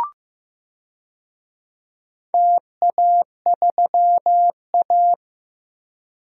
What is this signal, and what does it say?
Morse code, heard as a keyed tone, about 15 wpm, 700 Hz. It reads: TA3A
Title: DXCC-100 Call Signs - 4 Characters - 15wpm